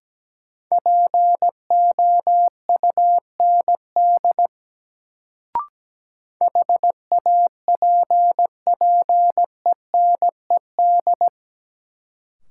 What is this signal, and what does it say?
Morse code, heard as a keyed tone, about 17 wpm, 700 Hz. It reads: POUND HAPPENED
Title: Sentences from Top 1000 Words - Review Part 2 - New Word Drill - 17wpm